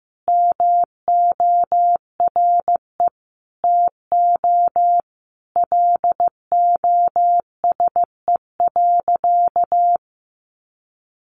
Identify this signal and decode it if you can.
Morse code, heard as a keyed tone, about 15 wpm, 700 Hz. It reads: MORE TO LOSE.
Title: Sentences from Top 1100 Words 15wpm